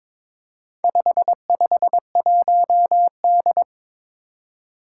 Morse code, heard as a keyed tone, about 22 wpm, 700 Hz. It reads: 551D
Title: Four Challenging Letter-Number Combinations 22wpm